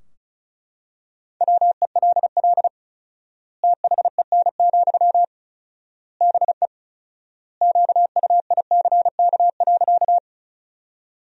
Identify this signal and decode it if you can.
Morse code, heard as a keyed tone, about 36 wpm, 700 Hz. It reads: WELL THEN, BE QUICK.
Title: 4x Word-Spacing - Sentences from Top 500 Words - 35wpm